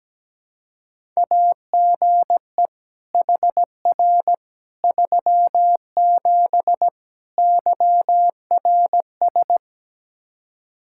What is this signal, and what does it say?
Morse code, heard as a keyed tone, about 17 wpm, 700 Hz. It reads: AGE HR 37 YRS
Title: Practice QSO - Exchange 3 - More Info - 17wpm